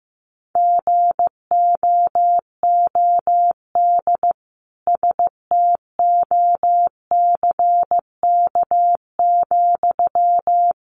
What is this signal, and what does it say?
Morse code, heard as a keyed tone, about 15 wpm, 700 Hz. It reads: GOOD STOCK,
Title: Sentences from Top 800 Words 15wpm